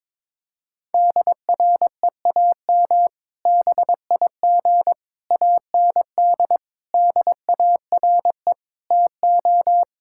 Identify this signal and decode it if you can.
Morse code, heard as a keyed tone, about 22 wpm, 700 Hz. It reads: DREAM BIG AND DARE TO
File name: Difficult Sentences with Hints - 22wpm